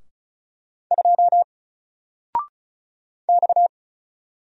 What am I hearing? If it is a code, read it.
Morse code, heard as a keyed tone, about 35 wpm, 700 Hz. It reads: 2 X
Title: Single Letter-Number Alt 35wpm